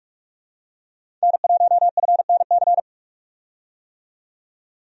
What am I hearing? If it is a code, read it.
Morse code, heard as a keyed tone, about 46 wpm, 700 Hz. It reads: N1FNC
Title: USA Call Signs - 5 Characters - 45wpm